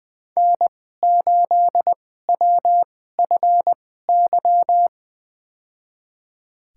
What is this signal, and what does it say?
Morse code, heard as a keyed tone, about 20 wpm, spaced as for 15 wpm, 700 Hz. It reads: N8WFY
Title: USA Call Signs - 5 Characters - 2x Character Spacing - 20wpm